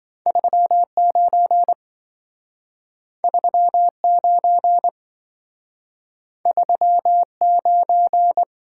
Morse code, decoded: 39 39 39

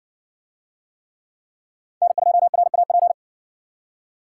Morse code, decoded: N2RAP

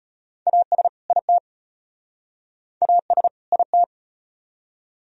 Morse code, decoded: AS IT AS IT

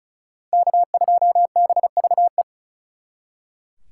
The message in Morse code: K2BVE

